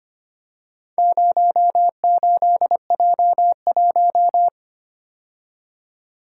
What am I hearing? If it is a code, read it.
Morse code, heard as a keyed tone, about 25 wpm, 700 Hz. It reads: 08J1
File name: practice / 25wpm / Four Letter-Number Combinations - 25wpm